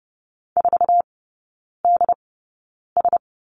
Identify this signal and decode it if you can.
Morse code, heard as a keyed tone, about 30 wpm, spaced as for 9 wpm, 700 Hz. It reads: 4DS
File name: Lesson 16e - Letter c - Three Characters 30wpm